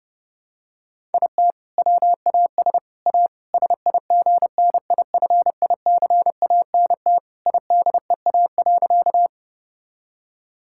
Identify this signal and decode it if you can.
Morse code, heard as a keyed tone, about 30 wpm, 700 Hz. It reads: IT WAS A SIGNIFICANT IDEA.
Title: Sentences from Top 1000 Words - Review Part 6 - 30wpm